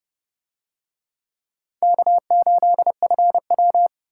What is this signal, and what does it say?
Morse code, heard as a keyed tone, about 29 wpm, 700 Hz. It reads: K8FW